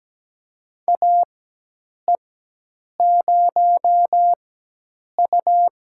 Morse code, heard as a keyed tone, about 17 wpm, spaced as for 8 wpm, 700 Hz. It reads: AE0U
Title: USA Call Signs - 4 Characters - 4x Character Spacing - 17wpm